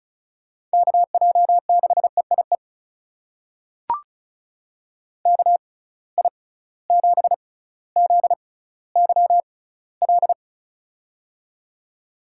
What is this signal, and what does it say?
Morse code, heard as a keyed tone, about 36 wpm, spaced as for 18 wpm, 700 Hz. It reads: KJ6EIE KI7ZYL
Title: USA Call Signs - 6 Characters - 6x Character Spacing - 35wpm